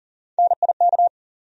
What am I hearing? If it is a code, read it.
Morse code, heard as a keyed tone, about 40 wpm, 700 Hz. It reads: NIK
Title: USA Call Sign Postfixes - 3 Letters 40wpm